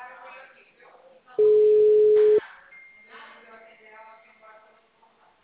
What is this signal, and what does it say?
Ringback tone (Europe, 425 Hz), with speech.